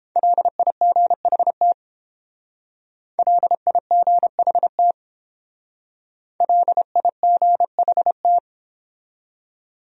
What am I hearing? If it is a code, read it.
Morse code, heard as a keyed tone, about 30 wpm, 700 Hz. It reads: LIGHT LIGHT LIGHT